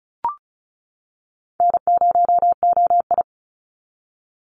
Morse code, heard as a keyed tone, about 34 wpm, 700 Hz. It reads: N0OI
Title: NA SOTA Activator Callsigns 35wpm